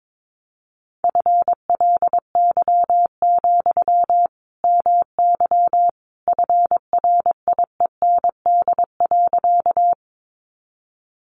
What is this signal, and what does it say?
Morse code, heard as a keyed tone, about 22 wpm, 700 Hz. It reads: FLY, MY FRIEND.
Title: Sentences from Top 500 Words 22wpm